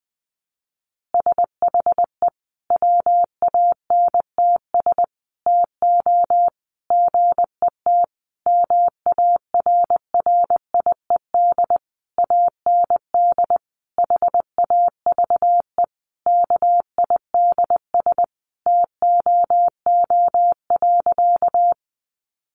Morse code, heard as a keyed tone, about 20 wpm, 700 Hz. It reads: SHE WANTS TO GET MARRIED AND HAVE KIDS TOO.
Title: Easy Phrases 20wpm